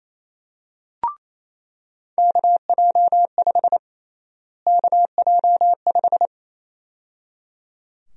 Morse code, heard as a keyed tone, about 28 wpm, 700 Hz. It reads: KJ5 KJ5
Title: USA Call Sign Prefixes - Repeated 2x - 28wpm